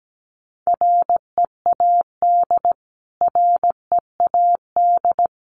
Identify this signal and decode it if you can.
Morse code, heard as a keyed tone, about 17 wpm, 700 Hz. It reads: READ READ
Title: Top 500 Words Repeated 17wpm